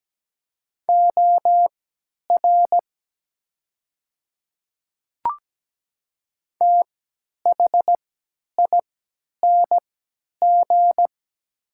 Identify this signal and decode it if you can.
Morse code, heard as a keyed tone, about 17 wpm, spaced as for 10 wpm, 700 Hz. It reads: OR THING
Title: 3x Character Spacing - Top 100 Words - 17wpm